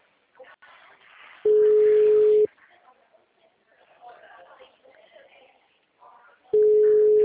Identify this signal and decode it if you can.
Ringback tone (Europe, 425 Hz), with speech.
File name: out- -2016- - - .36874